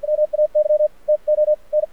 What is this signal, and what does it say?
Morse code, heard as a keyed tone, about 53 wpm, spaced as for 44 wpm, 600 Hz. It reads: MAY TO N
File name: archived / CRSnet